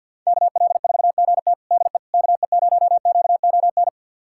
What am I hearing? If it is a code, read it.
Morse code, heard as a keyed tone, about 52 wpm, 700 Hz. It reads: KL4GT DE KE0QON
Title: USA Callsign DE Callsign 50wpm